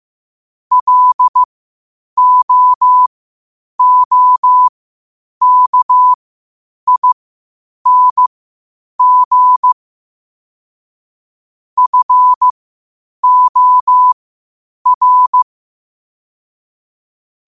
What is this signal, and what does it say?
Morse code, heard as a keyed tone, about 15 wpm, spaced as for 8 wpm, 1000 Hz. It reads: LOOKING FOR
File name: qotdfiles / 8wpm